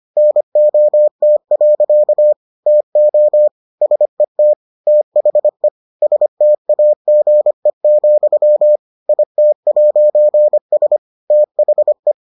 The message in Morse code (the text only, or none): NOT. TO SET THE STAGE, IT'S THE